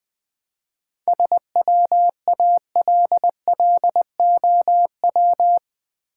Morse code, heard as a keyed tone, about 20 wpm, 700 Hz. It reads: SWALLOW
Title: 7 Letter Words - First 3 Letters Given - 20wpm